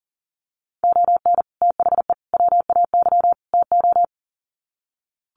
Morse code, heard as a keyed tone, about 40 wpm, 700 Hz. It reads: ON THE WAY TO
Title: Sets of 4 Words 40wpm